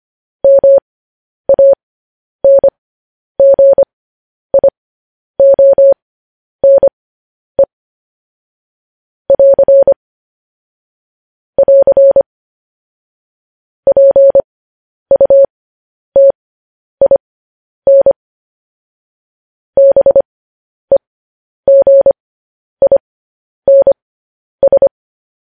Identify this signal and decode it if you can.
Morse code, heard as a keyed tone, about 25 wpm, spaced as for 10 wpm, 550 Hz. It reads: MANGIONE <AR> <AR> PUTIN BEGINS
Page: news headlines in morse code at 10 wpm